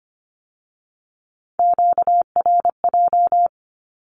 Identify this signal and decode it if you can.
Morse code, heard as a keyed tone, about 25 wpm, 700 Hz. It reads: QRJ
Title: Q-Codes with Spoken Meaning - 25wpm